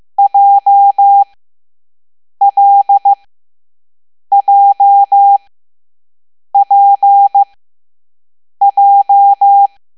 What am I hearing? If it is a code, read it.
Morse code, heard as a keyed tone, about 15 wpm, spaced as for 6 wpm, 800 Hz. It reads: JLJPJ
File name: Morse / Level7B